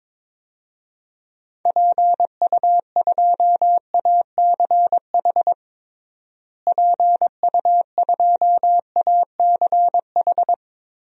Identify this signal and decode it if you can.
Morse code, heard as a keyed tone, about 21 wpm, 700 Hz. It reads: PU2ACH PU2ACH
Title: DXCC-100 Call Signs - 6 Characters - Repeated 3x - 22wpm